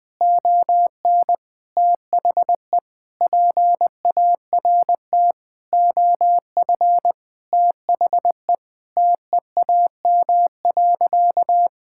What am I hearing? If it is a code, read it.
Morse code, heard as a keyed tone, about 20 wpm, 700 Hz. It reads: ON THE PART OF THE TEAM.
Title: Sentences with 4-Word Phrases - 20wpm